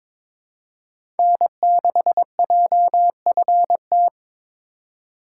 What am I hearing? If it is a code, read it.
Morse code, heard as a keyed tone, about 22 wpm, 700 Hz. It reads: N6JFT
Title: USA Call Signs - 5 Characters - 4x Character Spacing - 22wpm